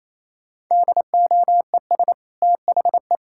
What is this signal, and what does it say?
Morse code, heard as a keyed tone, about 28 wpm, 700 Hz. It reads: DOES THE
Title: Sentences from Top 1600 Words 28wpm